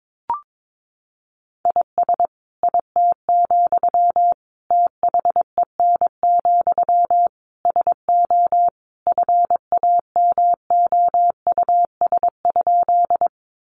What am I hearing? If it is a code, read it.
Morse code, heard as a keyed tone, about 22 wpm, 700 Hz. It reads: IS IT, THEN, SO FAMOUS?
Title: Sentences from Top 1500 Words 22wpm